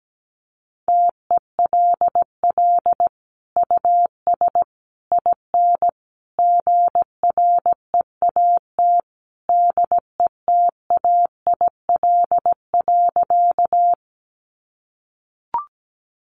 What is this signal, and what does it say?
Morse code, heard as a keyed tone, about 17 wpm, 700 Hz. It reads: TELL US IN GREAT DETAIL.
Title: Sentences from Top 1500 Words 17wpm